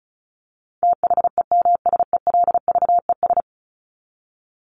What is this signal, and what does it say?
Morse code, heard as a keyed tone, about 34 wpm, 700 Hz. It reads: THEMSELVES
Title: Top 751-1000 Words 35wpm